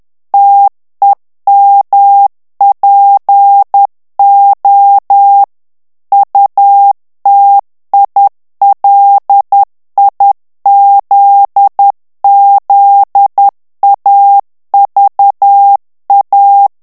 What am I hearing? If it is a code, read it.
Morse code, heard as a keyed tone, about 10 wpm, 800 Hz. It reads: TEMPO UTILIZZAVA
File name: morse mp3 / ITALIEN texte clair 1 vitesse 10 mots minute